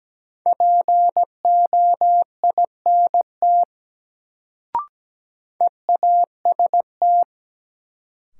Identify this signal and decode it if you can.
Morse code, heard as a keyed tone, about 17 wpm, 700 Hz. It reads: POINT EAST
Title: Sets of 3 Words - New Word Drill - 17wpm